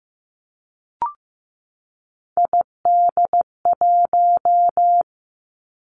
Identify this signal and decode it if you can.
Morse code, heard as a keyed tone, about 15 wpm, 700 Hz. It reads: ID1
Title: Three Challenging Letter-Number Combinations 15wpm